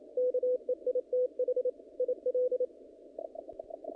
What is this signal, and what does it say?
Morse code, heard as a keyed tone, about 28 wpm, 500 Hz. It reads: KEITH IL
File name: contests / naqp-cw-k5tr-op-k5pi- -2000